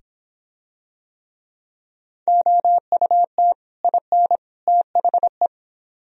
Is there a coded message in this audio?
Yes: Morse code from 2.3 to 5.5 s, heard as a keyed tone, about 26 wpm, 700 Hz. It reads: OUT IN THE